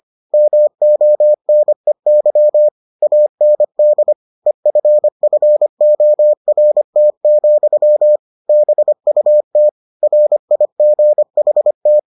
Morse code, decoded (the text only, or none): MONEY AND EFFORT, BUT RIGHT